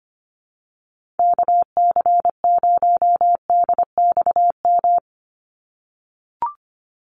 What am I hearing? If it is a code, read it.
Morse code, heard as a keyed tone, about 25 wpm, 700 Hz. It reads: KC0DXM
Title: USA Call Signs 25wpm